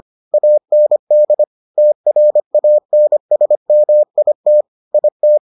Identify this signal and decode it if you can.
Morse code, heard as a keyed tone, about 25 wpm, 600 Hz. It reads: AND TRANSMIT IT